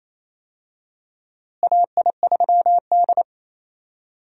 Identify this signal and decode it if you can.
Morse code, heard as a keyed tone, about 28 wpm, 700 Hz. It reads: AI3D
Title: USA Call Signs - 4 Characters - Repeated 3x - 28wpm